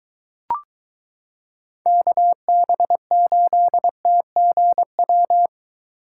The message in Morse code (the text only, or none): KB8TGW